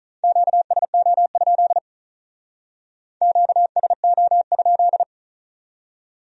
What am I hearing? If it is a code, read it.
Morse code, heard as a keyed tone, about 37 wpm, 700 Hz. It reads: QSO? QSO?
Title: Speed-Racing Q-Codes - 35wpm